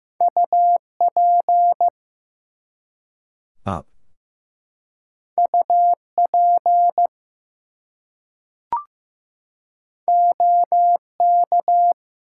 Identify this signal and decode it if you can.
Morse code, heard as a keyed tone, about 15 wpm, 700 Hz. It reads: UP UP OK
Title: Two Letter Words - 15wpm